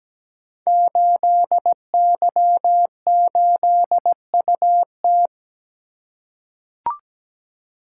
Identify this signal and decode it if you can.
Morse code, heard as a keyed tone, about 17 wpm, 700 Hz. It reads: 8Y8UT
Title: Code Groups - 5 Letter:Numbers - 17wpm